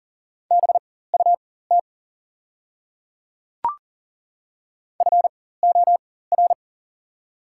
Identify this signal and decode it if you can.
Morse code, heard as a keyed tone, about 40 wpm, spaced as for 19 wpm, 690 Hz. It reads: BUT FOR
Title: 4x Character Spacing - Top 300 Words - 40wpm